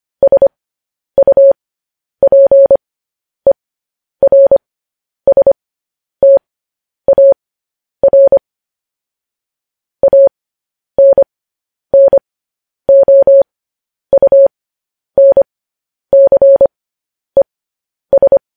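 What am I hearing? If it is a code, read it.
Morse code, heard as a keyed tone, about 25 wpm, spaced as for 10 wpm, 550 Hz. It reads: SUPERSTAR ANNOUNCES
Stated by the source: Morse code news headlines at 10 wpm.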